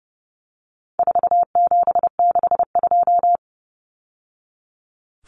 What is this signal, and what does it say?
Morse code, heard as a keyed tone, about 30 wpm, 700 Hz. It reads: 4762
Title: Numbers - 4-Digit 30wpm